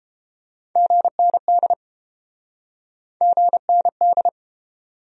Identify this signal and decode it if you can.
Morse code, heard as a keyed tone, about 32 wpm, 700 Hz. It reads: GND GND